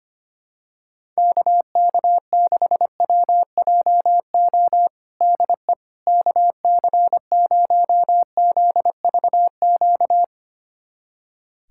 Morse code, heard as a keyed tone, about 25 wpm, 700 Hz. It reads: KK6WJO DE KC0ZVQ
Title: USA Callsign DE Callsign 25wpm